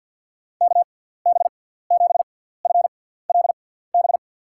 Morse code, heard as a keyed tone, about 53 wpm, spaced as for 17 wpm, 700 Hz. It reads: KB7FLB
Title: USA Call Signs - 6 Characters - 6x Character Spacing - 50wpm